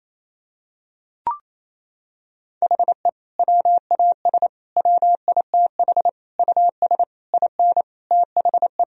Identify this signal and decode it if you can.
Morse code, heard as a keyed tone, about 28 wpm, 700 Hz. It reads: HE WAS WITH US IN THE
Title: Sentences from Top 1000 Words - Review Part 3 - 28wpm